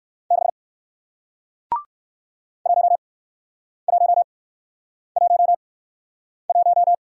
Morse code, heard as a keyed tone, about 53 wpm, spaced as for 9 wpm, 700 Hz. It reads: 6 1111
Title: Speed-Racing Single Letter-Number 45wpm